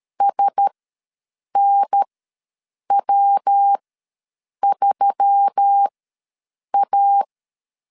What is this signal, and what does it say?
Morse code, heard as a keyed tone, about 13 wpm, spaced as for 7 wpm, 800 Hz. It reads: SNW3A
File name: LEZIONI / 7 GF9Z / 7 G ws 12wpm 3140